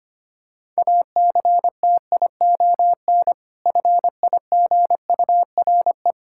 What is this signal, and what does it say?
Morse code, heard as a keyed tone, about 25 wpm, 700 Hz. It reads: ACTION FIGURE
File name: practice / 25wpm / Difficult Words with Hints - 25wpm